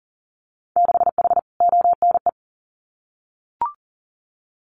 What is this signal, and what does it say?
Morse code, heard as a keyed tone, about 42 wpm, 700 Hz. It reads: 6H ONE
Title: Winter Field Day Exchanges Encore 40wpm